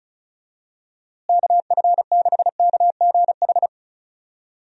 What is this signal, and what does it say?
Morse code, heard as a keyed tone, about 34 wpm, 700 Hz. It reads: KF6KGH
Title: USA Call Signs - 6 Characters - Repeated 3x - 35wpm